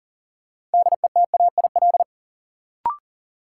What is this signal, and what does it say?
Morse code, heard as a keyed tone, about 40 wpm, 700 Hz. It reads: DETAIL
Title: Sentences from Top 1500 Words - New Word Drill - 40wpm